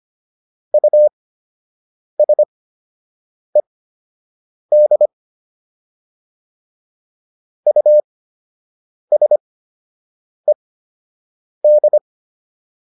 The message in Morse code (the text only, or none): USED USED